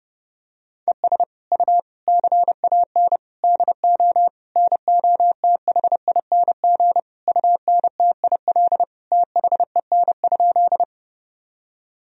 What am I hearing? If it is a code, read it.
Morse code, heard as a keyed tone, about 30 wpm, 700 Hz. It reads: ES U CAN DO NOTHING UNTIL THEN?
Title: Sentences from Top 500 Words and Abbreviations 30wpm